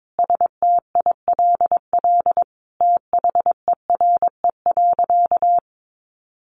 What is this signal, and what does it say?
Morse code, heard as a keyed tone, about 22 wpm, 700 Hz. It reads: STILL THERE.
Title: A Sound Of Thunder 22wpm